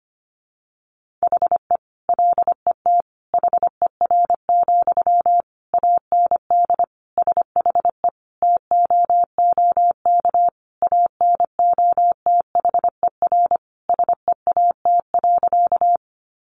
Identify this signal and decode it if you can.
Morse code, heard as a keyed tone, about 25 wpm, 700 Hz. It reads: HE LET HER, AND SHE TOOK ANOTHER SEAT.
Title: Sentences from Top 1500 Words 25wpm